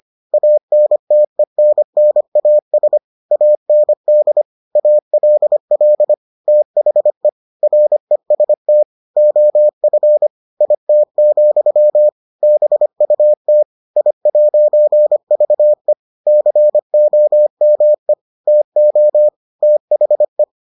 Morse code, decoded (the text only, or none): ANTENNAS AND ALL THE REST OF IT, BUT I'VE COME TO THE